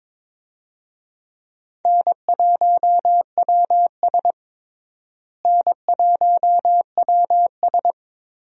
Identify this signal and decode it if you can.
Morse code, heard as a keyed tone, about 22 wpm, 700 Hz. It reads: N1WS N1WS
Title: USA Call Signs - 4 Characters - Repeated 3x - 22wpm